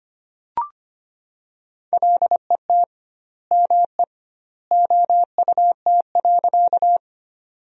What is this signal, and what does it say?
Morse code, heard as a keyed tone, about 25 wpm, 700 Hz. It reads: LET ME OUT.
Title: 2x Word-Spacing - Sentences from Top 300 Words - 25wpm